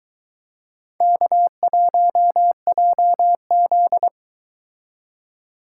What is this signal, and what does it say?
Morse code, heard as a keyed tone, about 23 wpm, 700 Hz. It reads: K1JZ